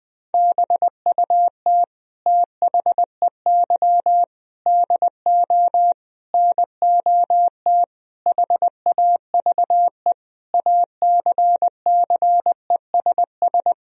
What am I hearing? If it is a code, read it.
Morse code, heard as a keyed tone, about 19 wpm, 700 Hz. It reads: BUT THEY DO NOT HAVE ACCESS
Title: Sentences from Top 1000 Words - Review Part 4 - 20wpm